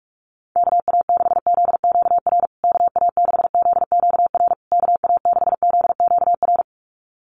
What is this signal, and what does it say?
Morse code, heard as a keyed tone, about 44 wpm, 700 Hz. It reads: KA6ZQR KA6ZQR KA6ZQR
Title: USA Call Signs Repeated 3x 45wpm